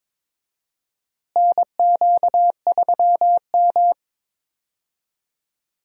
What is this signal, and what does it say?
Morse code, heard as a keyed tone, about 22 wpm, 700 Hz. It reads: NQ3M
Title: USA Call Signs - 4 Characters - Repeated 2x - 22wpm